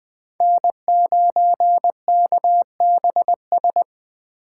Morse code, heard as a keyed tone, about 21 wpm, 700 Hz. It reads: N9KBS
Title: USA Call Signs - 5 Characters - 2x Character Spacing - 20wpm